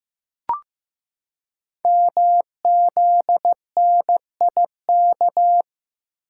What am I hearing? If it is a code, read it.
Morse code, heard as a keyed tone, about 15 wpm, 700 Hz. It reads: MZNIK
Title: Code Groups - 5 Letters - 15wpm